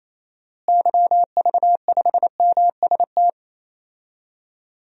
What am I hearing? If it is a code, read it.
Morse code, heard as a keyed tone, about 28 wpm, 700 Hz. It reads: YV5MST
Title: DXCC-100 Call Signs - 6 Characters - Repeated 2x - 28wpm